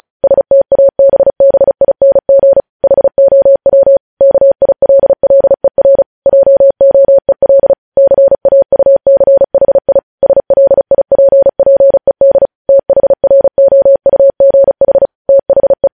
Morse code, heard as a keyed tone, about 36 wpm, 550 Hz. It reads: STABBING HOW KILLER JOEL CAUCHI SLIPPED THROUGH THE
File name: podcast / News headlines-1x-35x35